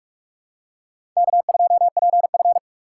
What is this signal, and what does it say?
Morse code, heard as a keyed tone, about 46 wpm, 690 Hz. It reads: K2PF
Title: USA Call Signs - 4 Characters - 8x Character Spacing - 45wpm